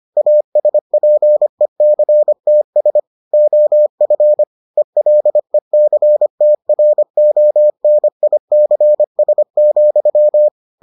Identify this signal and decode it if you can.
Morse code, heard as a keyed tone, about 25 wpm, 600 Hz. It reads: ASPECTS OF ELECTRONICS,